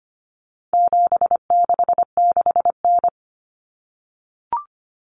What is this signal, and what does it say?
Morse code, heard as a keyed tone, about 25 wpm, 700 Hz. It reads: 766N
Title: Common Cut Numbers - 4-Digit 25wpm